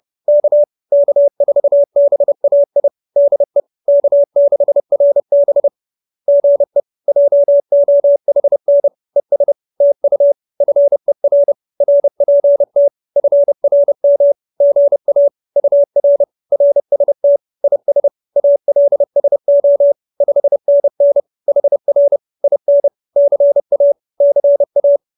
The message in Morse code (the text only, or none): K K4BAI DE K6RB GE JOHN ES TU FER RPT FRM GA UR RST IS ALSO 5NN HR IN CA CA